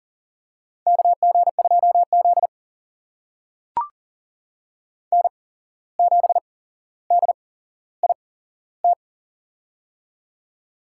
Morse code, heard as a keyed tone, about 40 wpm, spaced as for 16 wpm, 700 Hz. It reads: KG2Z N7DIT
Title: USA Call Signs - 8x Character Spacing - 40wpm